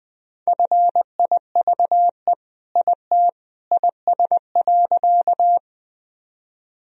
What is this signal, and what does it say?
Morse code, heard as a keyed tone, about 19 wpm, 700 Hz. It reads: FIVE IT IS.
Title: Sentences from Top 1000 Words - Review Part 2 - 20wpm